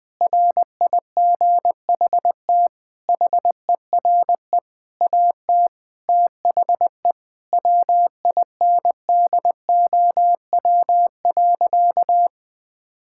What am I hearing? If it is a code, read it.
Morse code, heard as a keyed tone, about 20 wpm, 700 Hz. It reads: RIGHT HERE AT THE WINDOW.
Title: Sentences from Top 1600 Words 20wpm